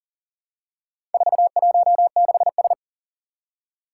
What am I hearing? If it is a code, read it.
Morse code, heard as a keyed tone, about 41 wpm, 700 Hz. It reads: 416S
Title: Four Challenging Letter-Number Combinations 40wpm